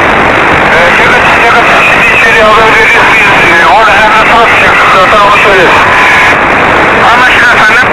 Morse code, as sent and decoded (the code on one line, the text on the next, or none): none